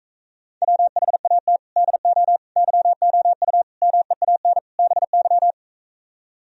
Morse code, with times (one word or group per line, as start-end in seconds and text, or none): 0.6-1.6 WHAT
1.8-2.4 DO
2.6-3.6 YOU
3.8-4.6 MEAN
4.8-5.5 BY